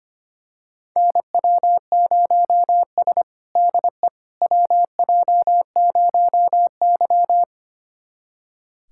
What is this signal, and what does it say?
Morse code, heard as a keyed tone, about 25 wpm, 700 Hz. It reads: NW0S DE WJ0Y
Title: USA Callsign DE Callsign - 4 Characters - 25wpm